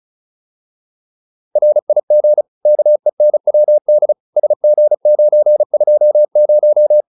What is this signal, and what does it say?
Morse code, heard as a keyed tone, about 35 wpm, 600 Hz. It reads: RIG KENWD SG920